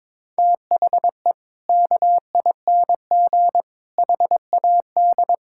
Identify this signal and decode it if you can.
Morse code, heard as a keyed tone, about 23 wpm, 700 Hz. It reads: THE KING HAD
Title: Sentences from Top 1000 Words - Review Part 6 - 22wpm